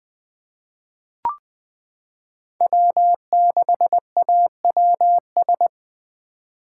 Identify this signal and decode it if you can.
Morse code, heard as a keyed tone, about 20 wpm, 700 Hz. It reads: W6AWS
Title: K1USN SST QSO Practice - 20wpm